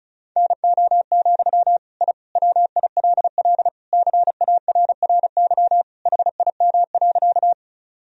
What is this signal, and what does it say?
Morse code, heard as a keyed tone, about 36 wpm, 700 Hz. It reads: NO, I WILL CARRY HIM.
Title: Sentences from Top 1000 Words - Review Part 1 - 35wpm